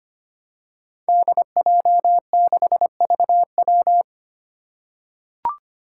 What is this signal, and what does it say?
Morse code, heard as a keyed tone, about 25 wpm, 700 Hz. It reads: DJ6VW
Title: Five Challenging Letter-Number Combinations 25wpm